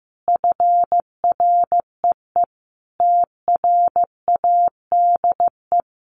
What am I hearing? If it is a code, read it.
Morse code, heard as a keyed tone, about 15 wpm, 700 Hz. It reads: FREE TRADE
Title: Sets of 5 Words Encore 15wpm